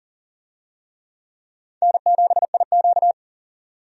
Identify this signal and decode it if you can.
Morse code, heard as a keyed tone, about 40 wpm, 700 Hz. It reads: N7IQ
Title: USA Call Signs - 4 Characters - 40wpm